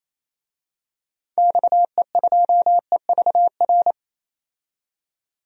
Morse code, heard as a keyed tone, about 28 wpm, 700 Hz. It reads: XE2EVR
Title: DXCC-100 Call Signs - 6 Characters - 28wpm